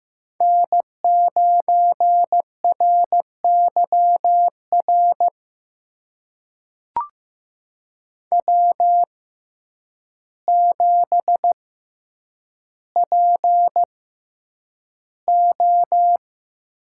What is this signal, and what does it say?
Morse code, heard as a keyed tone, about 15 wpm, spaced as for 8 wpm, 700 Hz. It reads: N9RYR W7PO
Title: USA Call Signs - 5 Characters - 6x Character Spacing - 15wpm